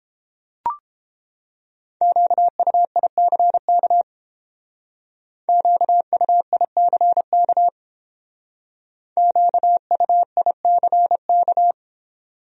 Morse code, heard as a keyed tone, about 30 wpm, 700 Hz. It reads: QUICK QUICK QUICK